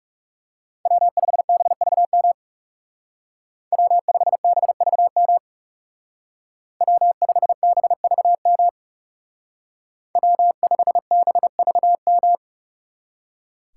Morse code, heard as a keyed tone, about 37 wpm, 700 Hz. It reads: W5BVM W5BVM W5BVM W5BVM
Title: Speed Racing - USA 1x3 Call Signs 30wpm